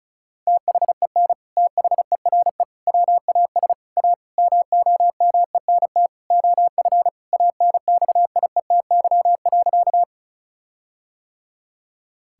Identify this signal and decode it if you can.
Morse code, heard as a keyed tone, about 34 wpm, 700 Hz. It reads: THEN THERE WAS A MOMENT OF ANXIETY.
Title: Sentences from Top 1600 Words 35wpm